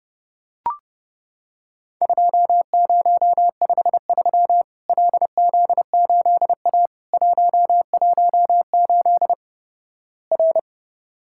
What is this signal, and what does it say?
Morse code, heard as a keyed tone, about 32 wpm, 700 Hz. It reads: 2053 LZ8A 118 R
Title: Worked All Europe DX Contest - QTC Exchange - 30wpm